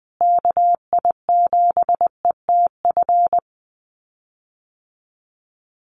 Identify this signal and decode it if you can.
Morse code, heard as a keyed tone, about 20 wpm, 700 Hz. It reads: KI7ETF
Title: USA Call Signs Repeated 3x 20wpm